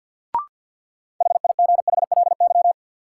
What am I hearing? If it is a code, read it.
Morse code, heard as a keyed tone, about 52 wpm, 700 Hz. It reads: HIGHLY
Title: Sentences from Top 1100 Words - New Word Drill - 50wpm